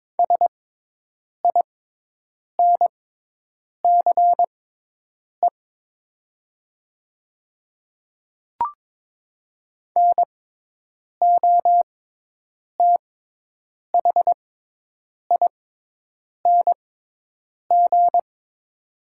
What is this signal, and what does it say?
Morse code, heard as a keyed tone, about 22 wpm, spaced as for 8 wpm, 700 Hz. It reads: SINCE NOTHING
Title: 6x Character Spacing - Top 500 Words - 22wpm